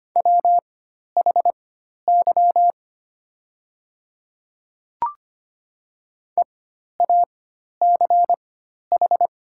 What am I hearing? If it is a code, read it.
Morse code, heard as a keyed tone, about 25 wpm, spaced as for 12 wpm, 700 Hz. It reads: WHY EACH
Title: 4x Character Spacing - Top 300 Words - 25wpm